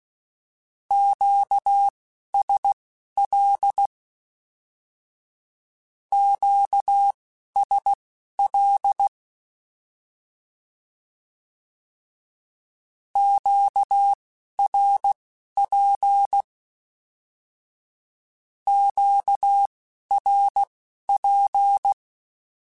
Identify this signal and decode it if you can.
Morse code, heard as a keyed tone, about 16 wpm, spaced as for 11 wpm, 800 Hz. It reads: QSL QSL QRP QRP
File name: Q Calls / Q Calls-mp3s / Q Calls Twice @10